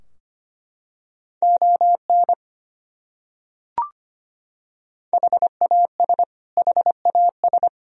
Morse code, heard as a keyed tone, about 25 wpm, 700 Hz. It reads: ON HAS HAS
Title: Top 100 Words Repeated 25wpm